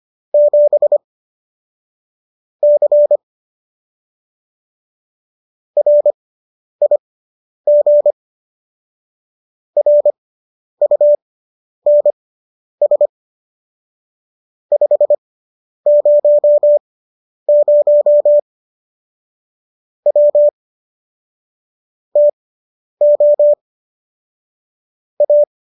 Morse code, heard as a keyed tone, about 25 wpm, spaced as for 10 wpm, 590 Hz. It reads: 7 C RIG RUNS 500 W TO A